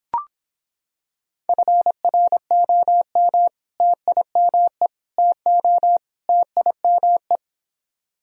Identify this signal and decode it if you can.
Morse code, heard as a keyed tone, about 26 wpm, 700 Hz. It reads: FROM TIME TO TIME